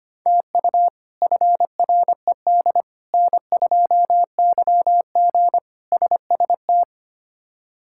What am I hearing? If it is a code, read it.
Morse code, heard as a keyed tone, about 25 wpm, 700 Hz. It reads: TU FRED N2YG SST
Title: K1USN SST QSO Practice - 25wpm